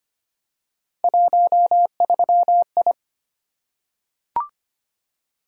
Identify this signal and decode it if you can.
Morse code, heard as a keyed tone, about 25 wpm, 700 Hz. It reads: 13I
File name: Three Letter-Number Combinations - 25wpm